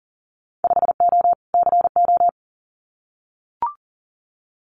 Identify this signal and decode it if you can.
Morse code, heard as a keyed tone, about 39 wpm, 700 Hz. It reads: 5O CO
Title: Winter Field Day Exchanges Encore 40wpm